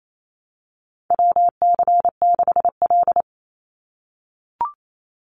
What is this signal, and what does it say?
Morse code, heard as a keyed tone, about 28 wpm, 700 Hz. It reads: WC6L
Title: USA Call Signs 28wpm